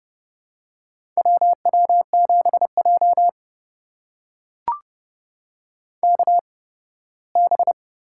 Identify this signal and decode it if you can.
Morse code, heard as a keyed tone, about 32 wpm, spaced as for 18 wpm, 700 Hz. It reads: WW7J KB
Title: USA Call Signs - 8x Character Spacing - 30wpm